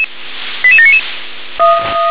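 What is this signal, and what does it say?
Five-tone selective call (ZVEI), heard as quick runs of five tones, tones of about 70 ms. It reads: 88799
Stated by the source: Five-tone ZVEI selective call received by SDR.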